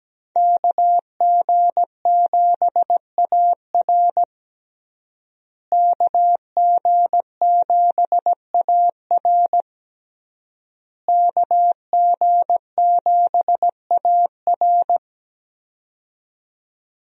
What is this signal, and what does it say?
Morse code, heard as a keyed tone, about 17 wpm, 700 Hz. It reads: KG7AR KG7AR KG7AR
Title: USA Call Signs - 5 Characters - Repeated 3x - 17wpm